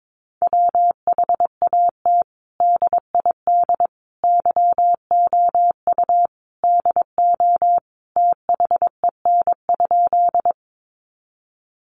Morse code, heard as a keyed tone, about 22 wpm, 700 Hz. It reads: WHAT DID YOU DO THEN?
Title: Sentences from Top 100 Words 22wpm